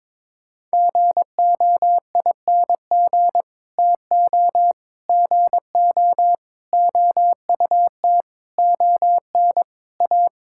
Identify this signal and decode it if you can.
Morse code, heard as a keyed tone, about 22 wpm, 700 Hz. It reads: GOING TO GO OUT ON A
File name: practice / 22wpm / Sets of 8 Words - 22wpm